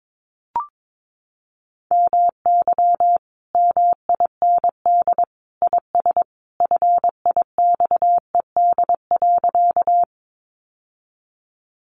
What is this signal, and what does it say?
Morse code, heard as a keyed tone, about 22 wpm, 700 Hz. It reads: MY MIND IS FIXED.
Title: Sentences from Top 1100 Words 22wpm